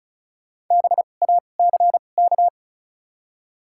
Morse code, heard as a keyed tone, about 34 wpm, spaced as for 25 wpm, 700 Hz. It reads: BACK
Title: 2x Character Spacing - Top 300 Words - 35wpm